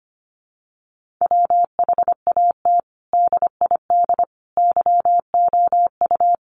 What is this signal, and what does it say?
Morse code, heard as a keyed tone, about 25 wpm, 700 Hz. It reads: WHAT DID YOU
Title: Sentences from Top 1500 Words 25wpm